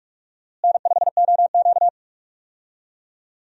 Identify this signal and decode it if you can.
Morse code, heard as a keyed tone, about 46 wpm, 700 Hz. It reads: N5OQ
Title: USA Call Signs - 4 Characters - Repeated 2x - 45wpm